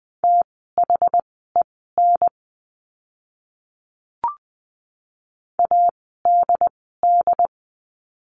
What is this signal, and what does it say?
Morse code, heard as a keyed tone, about 20 wpm, spaced as for 15 wpm, 700 Hz. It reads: THEN ADD
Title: 101 Lesson 17f - Letter u - Words 20wpm